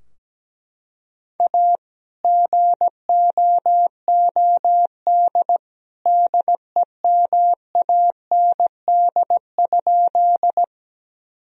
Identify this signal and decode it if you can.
Morse code, heard as a keyed tone, about 17 wpm, 700 Hz. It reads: A GOOD DEMAND?
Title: Sentences from Top 1000 Words - Review Part 4 - 17wpm